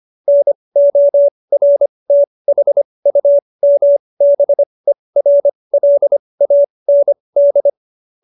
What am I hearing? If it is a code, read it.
Morse code, heard as a keyed tone, about 25 wpm, spaced as for 20 wpm, 570 Hz. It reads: NORTHUMBERLAND